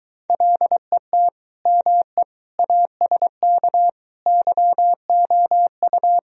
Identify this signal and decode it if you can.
Morse code, heard as a keyed tone, about 23 wpm, 700 Hz. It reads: LET ME ASK YOU